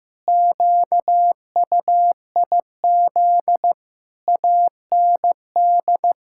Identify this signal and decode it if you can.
Morse code, heard as a keyed tone, about 15 wpm, 700 Hz. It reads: QUIZ AND
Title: Pangram Challenge v2 - 15wpm